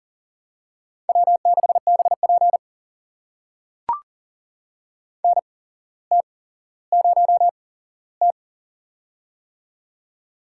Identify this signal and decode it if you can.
Morse code, heard as a keyed tone, about 42 wpm, spaced as for 17 wpm, 700 Hz. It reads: W6BP NT0T
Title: USA Call Signs - 4 Characters - 8x Character Spacing - 40wpm